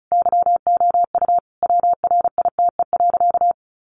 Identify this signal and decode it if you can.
Morse code, heard as a keyed tone, about 34 wpm, 700 Hz. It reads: YOU WRITE.
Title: Sentences from Top 800 Words 35wpm